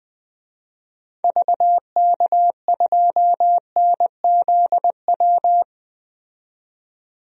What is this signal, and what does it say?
Morse code, heard as a keyed tone, about 20 wpm, 700 Hz. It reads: VK2NZW
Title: DXCC-100 Call Signs - 6 Characters - 20wpm